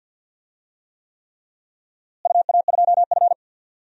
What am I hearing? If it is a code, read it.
Morse code, heard as a keyed tone, about 52 wpm, 700 Hz. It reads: UA2F
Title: DXCC-100 Call Signs - 4 Characters - Repeated 3x - 50wpm